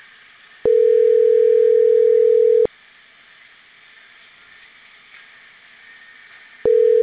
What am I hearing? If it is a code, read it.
Ringback tone (North America), 440 + 480 Hz.